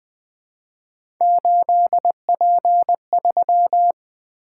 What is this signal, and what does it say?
Morse code, heard as a keyed tone, about 20 wpm, 700 Hz. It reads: 8P3
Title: Three Letter-Number Combinations - 20wpm